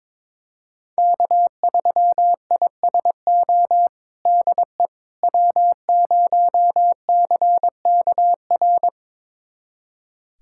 Morse code, heard as a keyed tone, about 22 wpm, 700 Hz. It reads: K3ISO DE W0CKR
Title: USA Callsign DE Callsign - 5 Characters - 22wpm